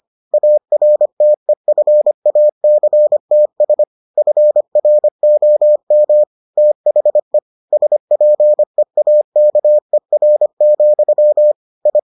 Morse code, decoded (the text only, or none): ARTEFACTS FROM THE SPEAKER, I